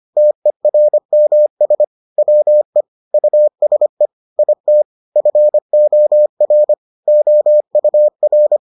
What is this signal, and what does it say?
Morse code, heard as a keyed tone, about 25 wpm, 600 Hz. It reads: TERMS WE USE IT FOR OUR